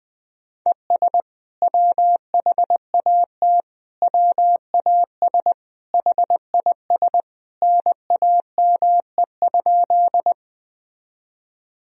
Morse code, heard as a keyed tone, about 21 wpm, 700 Hz. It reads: ES WHAT WAS HIS NAME?
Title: Sentences from Top 200 Words and Abbreviations 20wpm